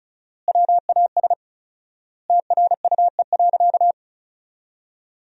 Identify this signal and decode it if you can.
Morse code, heard as a keyed tone, about 34 wpm, 700 Hz. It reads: WAS TRUE.
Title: 4x Word-Spacing - Sentences from Top 400 Words - 35wpm